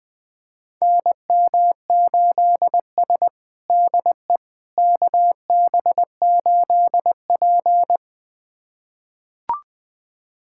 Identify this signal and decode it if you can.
Morse code, heard as a keyed tone, about 20 wpm, 700 Hz. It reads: NM8S DE KB8P
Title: USA Callsign DE Callsign - 4 Characters - 20wpm